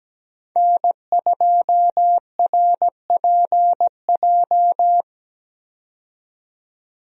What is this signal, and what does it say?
Morse code, heard as a keyed tone, about 17 wpm, 700 Hz. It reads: N2RPJ
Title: USA Call Signs - 5 Characters - 17wpm